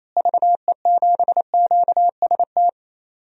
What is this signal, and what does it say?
Morse code, heard as a keyed tone, about 28 wpm, 700 Hz. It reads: VE7QST
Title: DXCC-100 Call Signs - 6 Characters - Repeated 3x - 28wpm